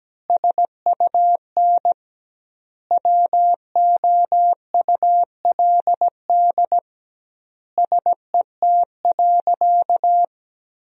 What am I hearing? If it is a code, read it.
Morse code, heard as a keyed tone, about 17 wpm, 700 Hz. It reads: SUN WOULD SET.
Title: 2x Word-Spacing - Sentences from Top 400 Words - 17wpm